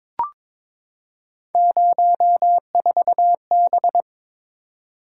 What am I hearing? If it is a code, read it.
Morse code, heard as a keyed tone, about 22 wpm, 700 Hz. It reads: 04B
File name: practice / 22wpm / Three Letter-Number Combinations - 22wpm